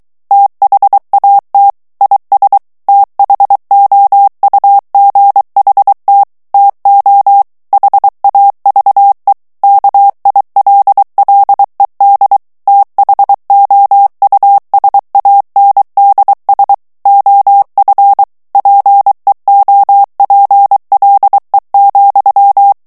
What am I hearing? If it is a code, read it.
Morse code, heard as a keyed tone, about 23 wpm, 800 Hz. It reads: THAT IS THOUGHT TO HAVE KILLED THOUSANDS OF PEOPLE,
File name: morse mp3 / ANGLAIS texte clair 2 vitesse 22 mots minute